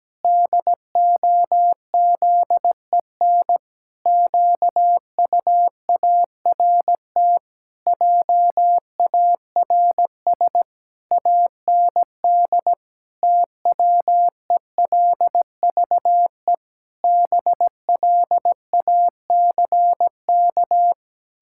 Morse code, heard as a keyed tone, about 17 wpm, 700 Hz. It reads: DOZEN QUART JARS AND TWELVE BLACK
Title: Pangram Challenge v2 - 17wpm